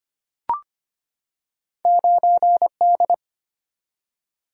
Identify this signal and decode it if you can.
Morse code, heard as a keyed tone, about 25 wpm, 700 Hz. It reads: 9D
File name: Two Letter-Number Combinations - 25wpm